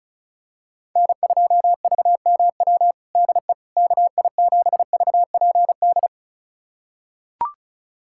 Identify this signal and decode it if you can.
Morse code, heard as a keyed tone, about 36 wpm, 700 Hz. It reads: N2VMW DE KI7VPD
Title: USA Callsign DE Callsign 35wpm